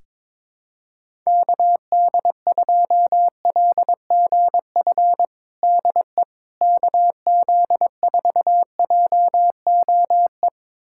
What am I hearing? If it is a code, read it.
Morse code, heard as a keyed tone, about 22 wpm, 700 Hz. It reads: KD2LGF DE KZ4JOE
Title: USA Callsign DE Callsign - 6 Characters - 22wpm